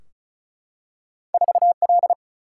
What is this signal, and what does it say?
Morse code, heard as a keyed tone, about 36 wpm, 700 Hz. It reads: 4L
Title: Single-Letter Single-Number 35wpm